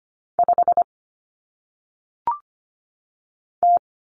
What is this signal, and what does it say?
Morse code, heard as a keyed tone, about 25 wpm, 700 Hz. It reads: 5 T
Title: Common Cut Numbers - 1-Digit 25wpm